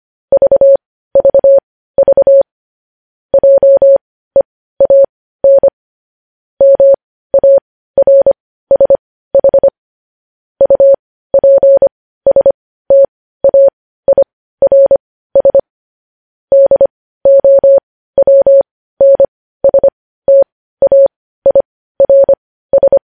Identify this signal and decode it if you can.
Morse code, heard as a keyed tone, about 25 wpm, spaced as for 15 wpm, 550 Hz. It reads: VVV JEAN MARSH UPSTAIRS DOWNSTAIRS
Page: news headlines in morse code at 15 wpm